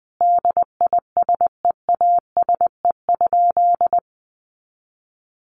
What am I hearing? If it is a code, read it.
Morse code, heard as a keyed tone, about 20 wpm, 700 Hz. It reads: DISEASE?
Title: Sentences from Top 1200 Words 20wpm